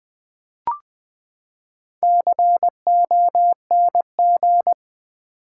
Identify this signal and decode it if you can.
Morse code, heard as a keyed tone, about 20 wpm, 700 Hz. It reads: CONG
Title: Build A Word - 4 Letters At A Time - 12-Letter Words - 20wpm